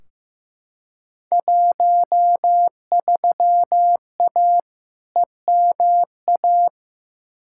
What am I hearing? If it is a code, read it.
Morse code, heard as a keyed tone, about 15 wpm, 700 Hz. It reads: 13A EMA
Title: Field Day Exchanges Encore - 15wpm